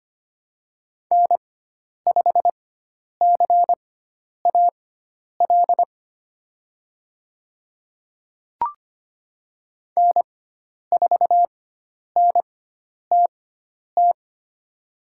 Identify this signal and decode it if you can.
Morse code, heard as a keyed tone, about 25 wpm, spaced as for 10 wpm, 700 Hz. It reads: N5CAL N4NTT
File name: Lesson 16g - Letter c - Call Signs 25wpm